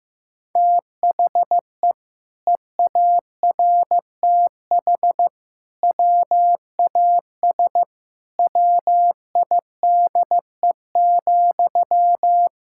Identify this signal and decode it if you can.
Morse code, heard as a keyed tone, about 16 wpm, 700 Hz. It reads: THE EARTH WAS WIDE,
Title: Sentences from Top 1000 Words - Review Part 6 - 15wpm